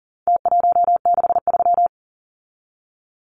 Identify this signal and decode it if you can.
Morse code, heard as a keyed tone, about 40 wpm, 700 Hz. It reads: T163
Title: Numbers - Contest Style - 4-Digit 40wpm